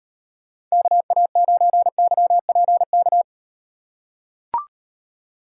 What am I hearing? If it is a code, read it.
Morse code, heard as a keyed tone, about 38 wpm, 700 Hz. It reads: KA9YPK